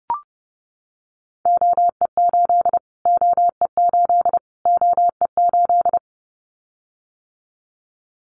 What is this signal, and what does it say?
Morse code, heard as a keyed tone, about 30 wpm, 700 Hz. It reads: OE8 OE8 OE8
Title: Callsign Prefixes for Top 25 DXCC - Repeated 3x - 30wpm